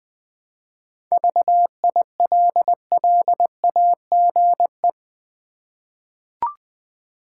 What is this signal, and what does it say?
Morse code, heard as a keyed tone, about 21 wpm, 700 Hz. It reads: VILLAGE
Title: 7 Letter Words - First 3 Letters Given - 20wpm